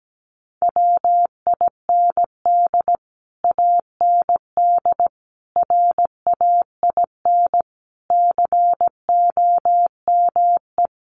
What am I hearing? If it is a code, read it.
Morse code, heard as a keyed tone, about 17 wpm, 700 Hz. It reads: WIND AND RAIN COME
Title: Sentences from Top 1400 Words 17wpm